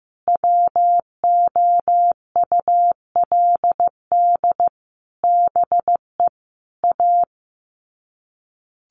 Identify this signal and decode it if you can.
Morse code, heard as a keyed tone, about 15 wpm, 700 Hz. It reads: WOULD BE A
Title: Sets of 5 Words Encore2 15wpm